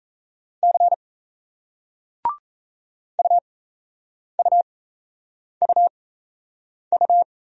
Morse code, heard as a keyed tone, about 38 wpm, spaced as for 8 wpm, 700 Hz. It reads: C UUUU